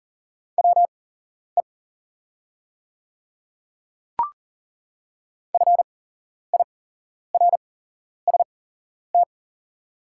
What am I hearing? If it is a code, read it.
Morse code, heard as a keyed tone, about 40 wpm, spaced as for 11 wpm, 690 Hz. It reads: WE FIRST
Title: 8x Character Spacing - Top 200 Words - 40wpm